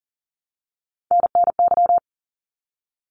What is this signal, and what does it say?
Morse code, heard as a keyed tone, about 40 wpm, 700 Hz. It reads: NNY
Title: ARRL Sections 40wpm